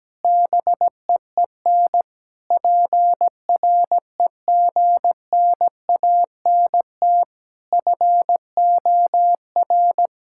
Morse code, heard as a keyed tone, about 17 wpm, 700 Hz. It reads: BEEN PREGNANT FOR
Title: Sentences from Top 1000 Words - Review Part 6 - 17wpm